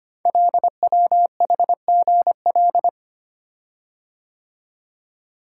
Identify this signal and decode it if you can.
Morse code, heard as a keyed tone, about 25 wpm, 700 Hz. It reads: LWHGL
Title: Code Groups - 5 Letter:Numbers - 25wpm